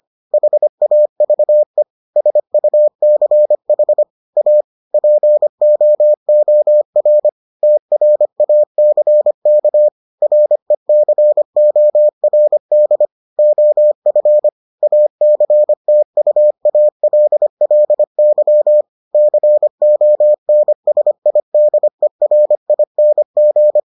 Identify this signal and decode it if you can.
Morse code, heard as a keyed tone, about 25 wpm, 600 Hz. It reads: HAVE SUCH A POOR TRACK RECORD OF ACTUALLY CONSIDERING